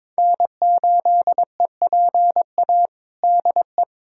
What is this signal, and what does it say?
Morse code, heard as a keyed tone, about 22 wpm, 700 Hz. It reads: N8EPA DE
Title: USA Callsign DE Callsign - 5 Characters - 22wpm